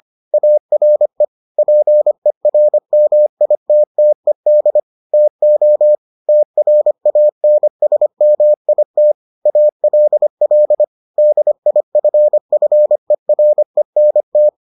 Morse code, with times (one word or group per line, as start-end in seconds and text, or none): 0.3-1.2 ARE
1.6-4.8 PERMITTED
5.1-5.9 TO
6.3-9.1 TRANSMIT
9.5-10.8 ALL
11.2-14.5 DIFFERENT